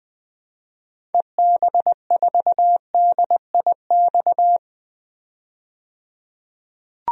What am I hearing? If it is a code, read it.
Morse code, heard as a keyed tone, about 20 wpm, 700 Hz. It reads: EB4DIX
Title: DXCC-100 Call Signs - 6 Characters - 20wpm